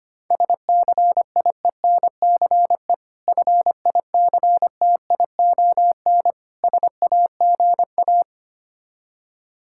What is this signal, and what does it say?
Morse code, heard as a keyed tone, about 25 wpm, 690 Hz. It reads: SCIENCE FICTION SAGA
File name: practice / 25wpm / Difficult Words with Hints - 25wpm